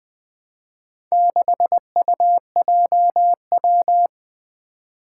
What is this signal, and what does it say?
Morse code, heard as a keyed tone, about 20 wpm, 700 Hz. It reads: 6UJW
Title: Four Challenging Letter-Number Combinations 20wpm